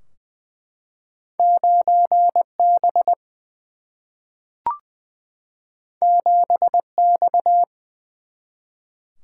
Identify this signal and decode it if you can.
Morse code, heard as a keyed tone, about 20 wpm, 700 Hz. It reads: 9B 7X
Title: Single-Letter Single-Number 20wpm